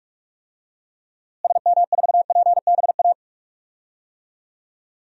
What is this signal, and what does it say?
Morse code, heard as a keyed tone, about 46 wpm, 700 Hz. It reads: SM4PBA
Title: DXCC-100 Call Signs - 6 Characters - Repeated 3x - 45wpm